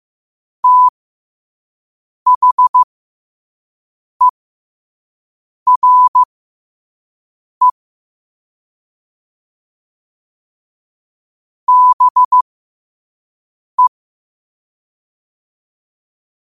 Morse code, heard as a keyed tone, about 15 wpm, spaced as for 5 wpm, 1000 Hz. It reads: THERE BE
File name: qotdfiles / 5wpm